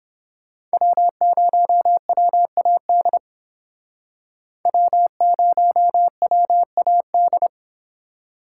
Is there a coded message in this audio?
Yes: Morse code from 0.7 to 7.5 s, heard as a keyed tone, about 30 wpm, 700 Hz. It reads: W0WAD W0WAD